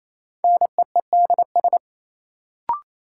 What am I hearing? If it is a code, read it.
Morse code, heard as a keyed tone, about 28 wpm, 700 Hz. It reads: NEEDS
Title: Sentences from Top 1000 Words - Review Part 3 - New Word Drill - 28wpm